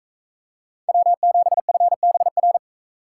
Morse code, heard as a keyed tone, about 41 wpm, 700 Hz. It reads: W7FBR